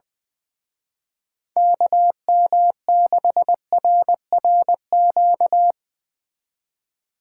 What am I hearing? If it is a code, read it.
Morse code, heard as a keyed tone, about 23 wpm, 700 Hz. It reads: KM6RRQ
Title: Speed Racing - USA Call Signs 20wpm